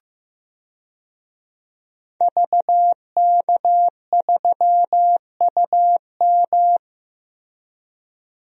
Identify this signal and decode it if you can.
Morse code, heard as a keyed tone, about 15 wpm, 700 Hz. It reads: VK3UM
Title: DXCC-100 Call Signs - 5 Characters - Repeated 3x - 15wpm